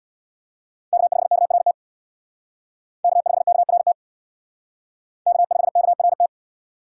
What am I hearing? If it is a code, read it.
Morse code, heard as a keyed tone, about 60 wpm, 700 Hz. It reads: K5KRT K5KRT K5KRT
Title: Speed Racing - USA 1x3 Call Signs 50wpm